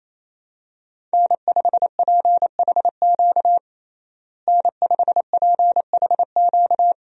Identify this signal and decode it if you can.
Morse code, heard as a keyed tone, about 28 wpm, 700 Hz. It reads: N5PHQ N5PHQ
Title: USA Call Signs - 5 Characters - Repeated 2x - 28wpm